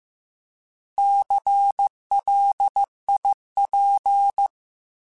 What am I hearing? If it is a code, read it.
Morse code, heard as a keyed tone, about 15 wpm, 800 Hz. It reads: CLIP
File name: Alphabetical / Alphabetical mp3s / Cs @13